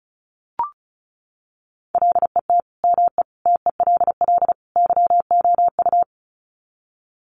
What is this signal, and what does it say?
Morse code, heard as a keyed tone, about 34 wpm, 700 Hz. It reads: LET ME TELL YOU
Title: Sets of 4 Words 35wpm